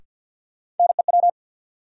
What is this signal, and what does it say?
Morse code, heard as a keyed tone, about 53 wpm, 700 Hz. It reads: NEW
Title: Sentences from Top 300 Words - New Word Drill - 50wpm